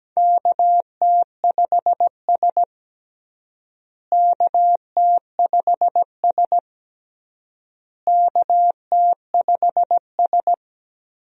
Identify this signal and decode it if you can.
Morse code, heard as a keyed tone, about 17 wpm, 700 Hz. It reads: KT5S KT5S KT5S
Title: USA Call Signs - 4 Characters - Repeated 3x - 17wpm